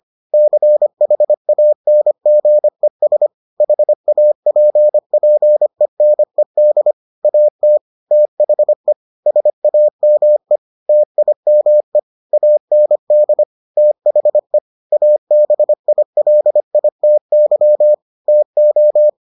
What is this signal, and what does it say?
Morse code, heard as a keyed tone, about 25 wpm, 600 Hz. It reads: CHANGES HAPPENED AT THE SAME TIME AND THE ABILITY TO